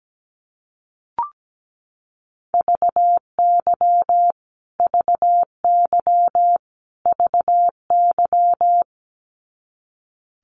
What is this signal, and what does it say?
Morse code, heard as a keyed tone, about 17 wpm, 700 Hz. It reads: VY VY VY
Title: Abbreviations Repeated 3x - 17wpm